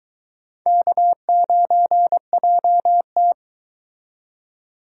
Morse code, heard as a keyed tone, about 23 wpm, 700 Hz. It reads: K9JT